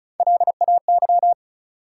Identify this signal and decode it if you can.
Morse code, heard as a keyed tone, about 36 wpm, 700 Hz. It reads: LAY
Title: Three Letter Words - 35wpm